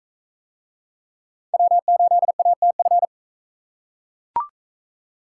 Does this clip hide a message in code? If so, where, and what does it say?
Yes: Morse code from 1.5 to 3.0 s, heard as a keyed tone, about 36 wpm, 700 Hz. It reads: W8ATF